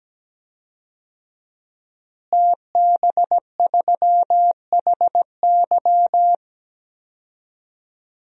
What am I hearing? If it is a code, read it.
Morse code, heard as a keyed tone, about 17 wpm, 700 Hz. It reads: TB3HY
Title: DXCC-100 Call Signs - 5 Characters - Repeated 3x - 17wpm